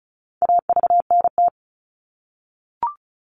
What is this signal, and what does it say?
Morse code, heard as a keyed tone, about 34 wpm, 700 Hz. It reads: AVNT
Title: All Cut Numbers - 4-Digit 35wpm